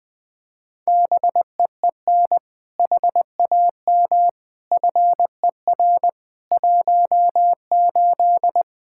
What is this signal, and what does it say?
Morse code, heard as a keyed tone, about 20 wpm, 700 Hz. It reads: BEEN HAM FER 18
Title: Practice QSO - Exchange 3 - More Info - 20wpm